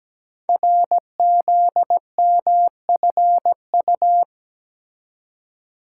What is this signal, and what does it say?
Morse code, heard as a keyed tone, about 17 wpm, 700 Hz. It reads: RZMFU
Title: Code Groups - 5 Letter:Numbers - 17wpm